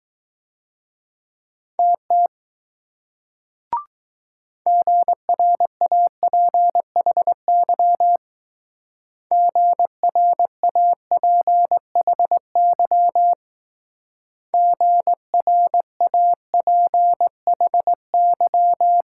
Morse code, heard as a keyed tone, about 20 wpm, 700 Hz. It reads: TT GRAPHY GRAPHY GRAPHY